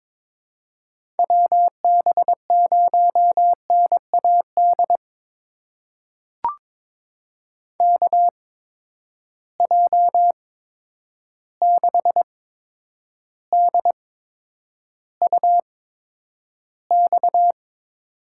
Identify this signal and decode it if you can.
Morse code, heard as a keyed tone, about 22 wpm, spaced as for 9 wpm, 700 Hz. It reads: WB0NAD KJ6DUX
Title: USA Call Signs - 6 Characters - 8x Character Spacing - 22wpm